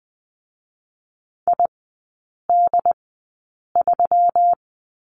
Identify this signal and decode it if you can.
Morse code, heard as a keyed tone, about 20 wpm, spaced as for 8 wpm, 700 Hz. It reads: ID3
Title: 184 Lesson 30e - Letter b - Three Characters 20wpm